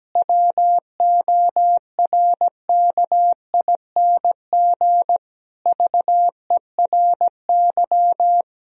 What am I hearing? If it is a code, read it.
Morse code, heard as a keyed tone, about 17 wpm, 700 Hz. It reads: WORKING VERY
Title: Sentences from Top 1000 Words - Review Part 4 - 17wpm